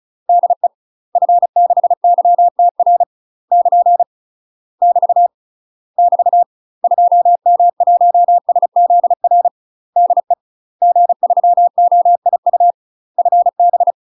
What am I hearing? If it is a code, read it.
Morse code, heard as a keyed tone, about 35 wpm, 700 Hz. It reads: DE F6YTR <KN> <BT> <BT> 2M1SZR DE G3OIU FB
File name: UK MASTER QSOs TRAINING / 35WPM / UK MASTER QSOs 0002